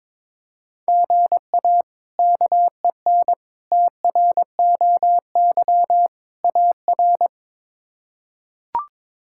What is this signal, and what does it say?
Morse code, heard as a keyed tone, about 22 wpm, 700 Hz. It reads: GA KEN TROY AR
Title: K1USN SST QSO Practice - 22wpm